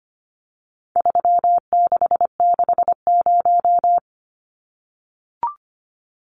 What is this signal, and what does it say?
Morse code, heard as a keyed tone, about 25 wpm, 700 Hz. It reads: 3660
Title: Numbers - 4-Digit 25wpm